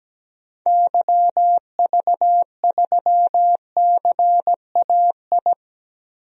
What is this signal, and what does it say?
Morse code, heard as a keyed tone, about 17 wpm, 700 Hz. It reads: YV3CAI
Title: DXCC-100 Call Signs - 6 Characters - Repeated 3x - 17wpm